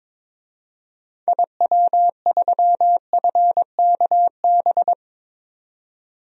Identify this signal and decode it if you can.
Morse code, heard as a keyed tone, about 21 wpm, 700 Hz. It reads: IW3FKB
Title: DXCC-100 Call Signs - 6 Characters - 22wpm